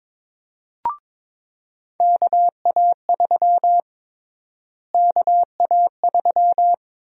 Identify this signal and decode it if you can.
Morse code, heard as a keyed tone, about 22 wpm, 700 Hz. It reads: KA3 KA3
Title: USA Call Sign Prefixes - Repeated 2x - 22wpm